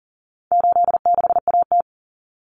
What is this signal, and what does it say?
Morse code, heard as a keyed tone, about 38 wpm, 700 Hz. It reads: 86AT
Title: Common Cut Numbers - 4-Digit 40wpm